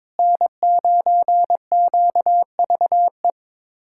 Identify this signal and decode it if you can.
Morse code, heard as a keyed tone, about 22 wpm, 700 Hz. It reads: N9QVE
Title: USA Callsign DE Callsign 22wpm